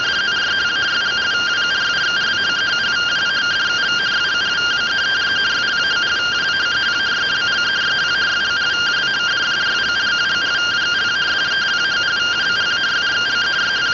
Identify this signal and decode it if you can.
Two-tone teleprinter signal, about 170 Hz shift.